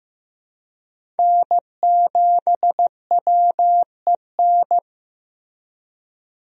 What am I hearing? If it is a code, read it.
Morse code, heard as a keyed tone, about 15 wpm, 700 Hz. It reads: N7WEN
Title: USA Call Signs - 5 Characters - Repeated 3x - 15wpm